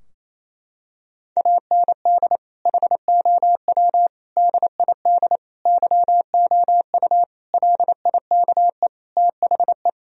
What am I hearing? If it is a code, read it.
Morse code, heard as a keyed tone, about 28 wpm, 700 Hz. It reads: AND HOW DID YOU LIKE THE
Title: Sentences from Top 1000 Words - Review Part 4 - 28wpm